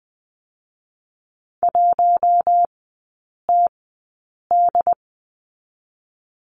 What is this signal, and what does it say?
Morse code, heard as a keyed tone, about 20 wpm, spaced as for 8 wpm, 700 Hz. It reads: 1TD
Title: 087 Lesson 15e - Number 5 - Three Characters 20wpm